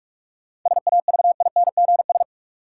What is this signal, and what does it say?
Morse code, heard as a keyed tone, about 44 wpm, 700 Hz. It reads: SAVINGS
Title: Sentences from Top 1200 Words - New Word Drill Alt - 45wpm